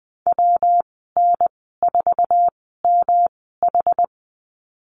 Morse code, heard as a keyed tone, about 20 wpm, spaced as for 15 wpm, 700 Hz. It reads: WN4MH
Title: 122 Lesson 20g - Number 3 - Call Signs 20wpm